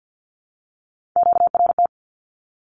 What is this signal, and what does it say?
Morse code, heard as a keyed tone, about 48 wpm, 700 Hz. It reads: QRT
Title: Q-Codes Repeated 3x 50wpm